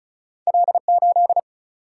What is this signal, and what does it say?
Morse code, heard as a keyed tone, about 35 wpm, 700 Hz. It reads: L8
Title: Single-Letter Single-Number 35wpm